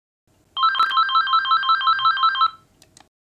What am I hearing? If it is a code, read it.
There is no code.